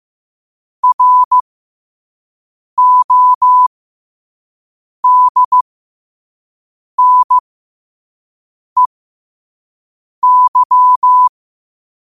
Morse code, heard as a keyed tone, about 15 wpm, spaced as for 5 wpm, 1000 Hz. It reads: RODNEY
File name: qotdfiles / 5wpm